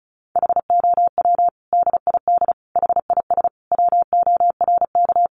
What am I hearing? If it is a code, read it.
Morse code, heard as a keyed tone, about 34 wpm, 700 Hz. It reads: HOW DID HIS WORK
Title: Sentences from Top 900 Words 35wpm